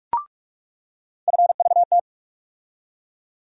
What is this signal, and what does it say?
Morse code, heard as a keyed tone, about 46 wpm, 700 Hz. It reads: FVT
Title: USA Call Sign Postfixes - 3 Letters 45wpm